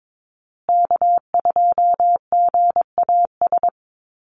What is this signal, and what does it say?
Morse code, heard as a keyed tone, about 22 wpm, 700 Hz. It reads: K2GAS
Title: USA Call Signs Repeated 2x 22wpm